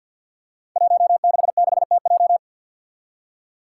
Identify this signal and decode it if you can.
Morse code, heard as a keyed tone, about 51 wpm, 700 Hz. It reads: 166TJ
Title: Five Challenging Letter-Number Combinations 50wpm